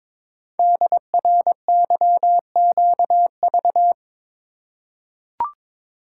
Morse code, heard as a keyed tone, about 23 wpm, 690 Hz. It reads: DRYQV
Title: Code Groups - 5 Letters - 22wpm